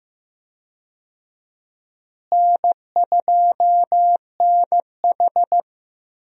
Morse code, heard as a keyed tone, about 15 wpm, 700 Hz. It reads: N2NH
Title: USA Call Signs - 4 Characters - Repeated 2x - 15wpm